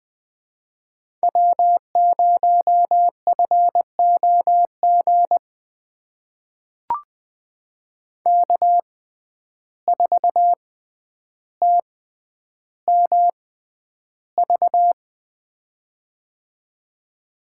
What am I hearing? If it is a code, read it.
Morse code, heard as a keyed tone, about 20 wpm, spaced as for 10 wpm, 700 Hz. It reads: W0FOG K4TMV
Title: USA Call Signs - 5 Characters - 6x Character Spacing - 20wpm